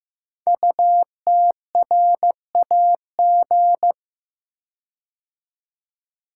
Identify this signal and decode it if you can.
Morse code, heard as a keyed tone, about 15 wpm, 700 Hz. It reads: UTRAG
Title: Code Groups - 5 Letter:Numbers - 15wpm